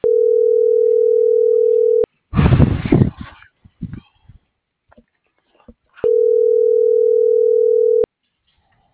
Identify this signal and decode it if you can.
Ringback tone (North America), 440 + 480 Hz.